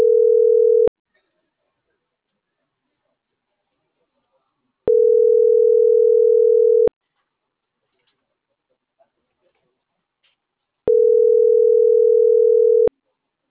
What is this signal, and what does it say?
Ringback tone (North America), 440 + 480 Hz.